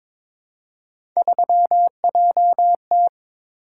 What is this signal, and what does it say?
Morse code, heard as a keyed tone, about 22 wpm, 700 Hz. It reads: 3JT
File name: practice / 22wpm / Three Letter-Number Combinations - 22wpm